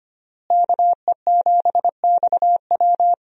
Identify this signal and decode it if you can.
Morse code, heard as a keyed tone, about 25 wpm, 700 Hz. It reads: KE7XW
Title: USA Call Signs - 5 Characters - Repeated 2x - 25wpm